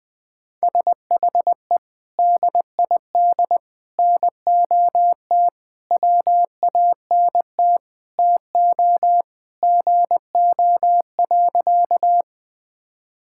Morse code, heard as a keyed tone, about 19 wpm, 700 Hz. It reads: SHE DID NOT WANT TO GO.
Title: Sentences with 4-Word Phrases - Spoken Beginning - 20wpm